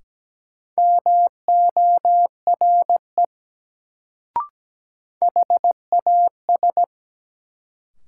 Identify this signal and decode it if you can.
Morse code, heard as a keyed tone, about 17 wpm, 700 Hz. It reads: MORE HAS
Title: Sets of 2 Words - New Word Drill - 17wpm